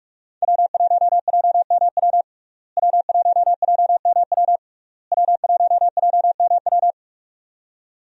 Morse code, heard as a keyed tone, about 46 wpm, 700 Hz. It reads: W1JMW W1JMW W1JMW
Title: USA Call Signs - 5 Characters - Repeated 3x - 45wpm